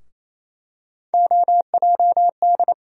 Morse code, heard as a keyed tone, about 27 wpm, 700 Hz. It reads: OJD
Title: USA Call Sign Postfixes - 3 Letters 28wpm